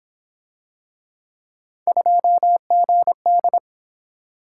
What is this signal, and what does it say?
Morse code, heard as a keyed tone, about 27 wpm, 700 Hz. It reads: 2GD
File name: Speed-Racing - Three Letter-Number Combinations - 22wpm